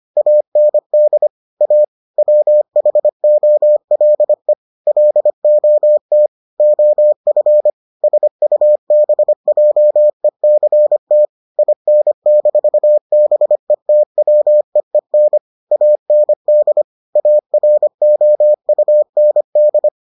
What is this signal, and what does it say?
Morse code, heard as a keyed tone, about 25 wpm, 600 Hz. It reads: AND A WHOLE LOT OF SUBJECT IN-BETWEEN AND AROUND